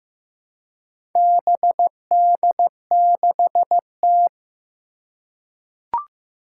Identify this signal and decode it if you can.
Morse code, heard as a keyed tone, about 15 wpm, 700 Hz. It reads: BD6T
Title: Four Challenging Letter-Number Combinations 15wpm